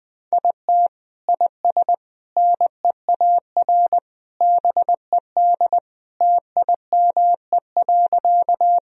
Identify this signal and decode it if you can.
Morse code, heard as a keyed tone, about 21 wpm, 700 Hz. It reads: IT IS NEAR BED TIME.
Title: Sentences from Top 1000 Words - Review Part 1 - 20wpm